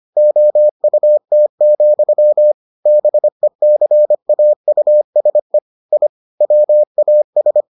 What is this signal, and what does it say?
Morse code, heard as a keyed tone, about 25 wpm, 600 Hz. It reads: OUT, BECAUSE I WAS